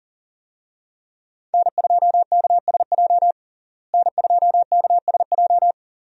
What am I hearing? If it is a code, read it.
Morse code, heard as a keyed tone, about 40 wpm, 700 Hz. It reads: N2KSJ N2KSJ
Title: USA Call Signs - 5 Characters - Repeated 3x - 40wpm